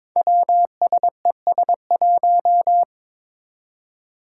Morse code, heard as a keyed tone, about 22 wpm, 700 Hz. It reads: WSES1
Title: Five Challenging Letter-Number Combinations 22wpm